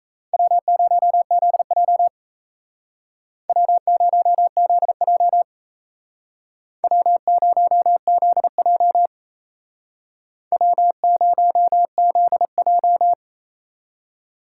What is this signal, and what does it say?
Morse code, heard as a keyed tone, about 35 wpm, 700 Hz. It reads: W0ZJ W0ZJ W0ZJ W0ZJ